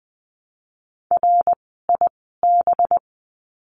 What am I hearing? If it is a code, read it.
Morse code, heard as a keyed tone, about 20 wpm, spaced as for 15 wpm, 700 Hz. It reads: RIB
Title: 185 Lesson 30f - Letter b - Words 20wpm